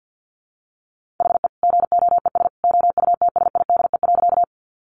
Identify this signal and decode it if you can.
Morse code, heard as a keyed tone, about 50 wpm, 700 Hz. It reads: HE GOES OUTSIDE.
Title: Easy Phrases 50wpm